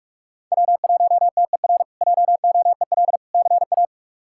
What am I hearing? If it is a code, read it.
Morse code, heard as a keyed tone, about 46 wpm, 700 Hz. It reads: W1TER JOEL CA
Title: K1USN SST QSO Practice - 45wpm